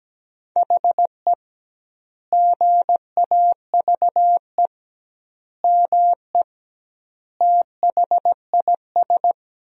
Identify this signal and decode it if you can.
Morse code, heard as a keyed tone, about 17 wpm, 700 Hz. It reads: HE GAVE ME THIS
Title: 2x Word-Spacing - Sentences from Top 500 Words - 17wpm